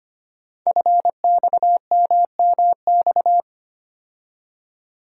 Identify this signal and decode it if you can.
Morse code, heard as a keyed tone, about 25 wpm, 700 Hz. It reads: FXMMX
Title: Code Groups - 5 Letters - 25wpm